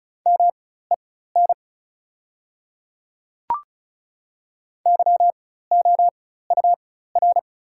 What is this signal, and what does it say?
Morse code, heard as a keyed tone, about 36 wpm, spaced as for 16 wpm, 690 Hz. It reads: MEN YOUR
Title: 4x Character Spacing - Top 300 Words - 35wpm